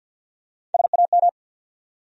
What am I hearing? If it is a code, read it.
Morse code, heard as a keyed tone, about 49 wpm, 700 Hz. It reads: SAM
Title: USA Common Names - Max 3 Letters - 50wpm